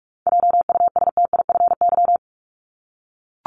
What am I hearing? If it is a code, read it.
Morse code, heard as a keyed tone, about 44 wpm, 700 Hz. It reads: JUSTIFY
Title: Top 1751-2000 Words 45wpm